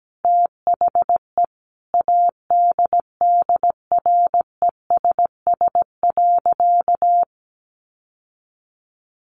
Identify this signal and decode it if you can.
Morse code, heard as a keyed tone, about 17 wpm, 700 Hz. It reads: THE ADDRESS.
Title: Sentences from Top 1000 Words 17wpm